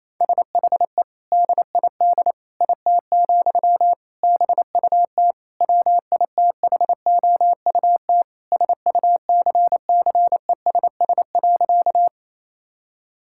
Morse code, heard as a keyed tone, about 28 wpm, 700 Hz. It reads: SHE DID IT, BUT WITHOUT SUCCESS.
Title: Sentences from Top 1000 Words - Review Part 6 - 28wpm